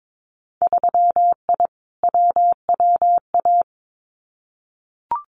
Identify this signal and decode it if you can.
Morse code, heard as a keyed tone, about 22 wpm, 700 Hz. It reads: 3I WWA
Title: Winter Field Day Exchanges Encore 22wpm